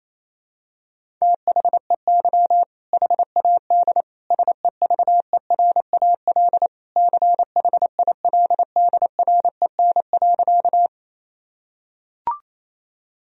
Morse code, heard as a keyed tone, about 27 wpm, 700 Hz. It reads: THEY HAD SEVERAL CHILDREN.
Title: Sentences from Top 1000 Words - Review Part 2 - 28wpm